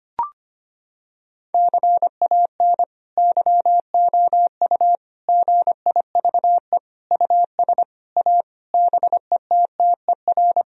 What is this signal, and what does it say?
Morse code, heard as a keyed tone, about 25 wpm, 700 Hz. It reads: CAN YOU GIVE US A BETTER
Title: Sentences from Top 1000 Words - Review Part 4 - 25wpm